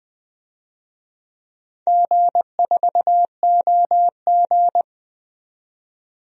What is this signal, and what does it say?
Morse code, heard as a keyed tone, about 20 wpm, 700 Hz. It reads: G4OG
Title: DXCC-100 Call Signs - 4 Characters - 20wpm